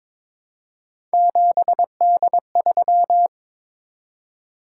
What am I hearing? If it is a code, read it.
Morse code, heard as a keyed tone, about 22 wpm, 700 Hz. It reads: 7D3
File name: practice / 22wpm / Three Letter-Number Combinations - 22wpm